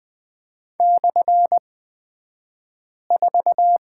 Morse code, heard as a keyed tone, about 20 wpm, spaced as for 5 wpm, 690 Hz. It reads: /4
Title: 228 Lesson 37d - Letter x - Two Characters 20wpm